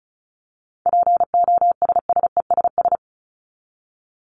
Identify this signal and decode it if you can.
Morse code, heard as a keyed tone, about 34 wpm, 700 Hz. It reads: POSSESS
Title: Top 1751-2000 Words 35wpm